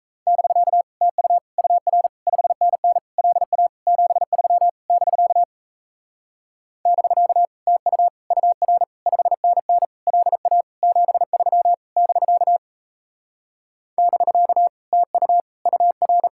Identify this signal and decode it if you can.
Morse code, heard as a keyed tone, about 38 wpm, 700 Hz. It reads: <BK> TU UR 5NN LA 73 <BK> <BK> TU UR 5NN LA 73 <BK> <BK> TU UR